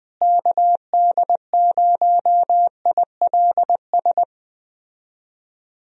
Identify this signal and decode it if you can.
Morse code, heard as a keyed tone, about 21 wpm, 700 Hz. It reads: KD0ILS
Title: USA Call Signs - 6 Characters - Repeated 3x - 20wpm